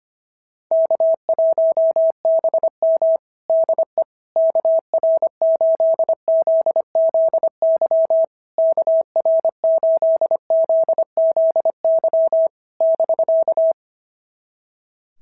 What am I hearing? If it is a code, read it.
Morse code, heard as a keyed tone, about 25 wpm, 690 Hz. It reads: K1BM DE KR8ZZY KR8ZZY <BK>
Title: SOTA QSO Practice 25wpm